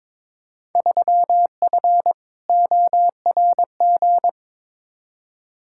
Morse code, heard as a keyed tone, about 22 wpm, 700 Hz. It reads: 3F ORG
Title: Field Day Exchanges - 22wpm